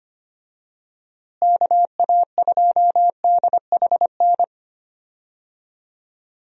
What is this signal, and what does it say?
Morse code, heard as a keyed tone, about 25 wpm, 700 Hz. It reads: KA2DHN
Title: USA Call Signs - 6 Characters - Repeated 2x - 25wpm